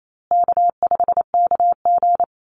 Morse code, heard as a keyed tone, about 28 wpm, 700 Hz. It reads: K5KG
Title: Easy Call Signs Repeated 3x - 28wpm